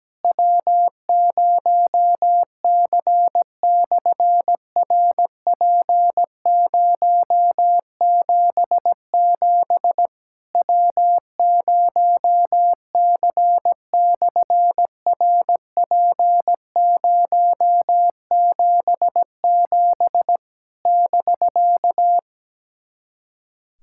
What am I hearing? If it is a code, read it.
Morse code, heard as a keyed tone, about 17 wpm, 700 Hz. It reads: W0C/RP077 W0C/RP077 <BK>
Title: SOTA QSO Practice 17wpm